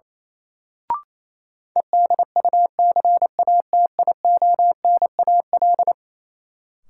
Morse code, heard as a keyed tone, about 28 wpm, 700 Hz. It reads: EDUCATIONAL
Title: Sentences from Top 1200 Words - New Word Drill - 28wpm